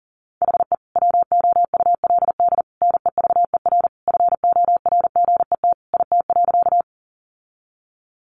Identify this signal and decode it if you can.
Morse code, heard as a keyed tone, about 40 wpm, 700 Hz. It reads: HE WOULD NEVER FORGET IT.
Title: Sentences from Top 1200 Words 40wpm